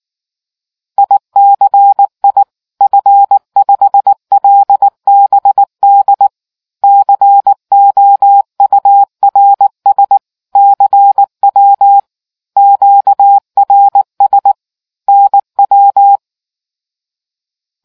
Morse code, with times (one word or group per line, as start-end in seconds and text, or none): 1.0-2.4 ICI
2.8-6.3 F5LBD
6.8-10.2 COURS
10.6-12.0 CW
12.6-14.5 QRS
15.1-16.2 NW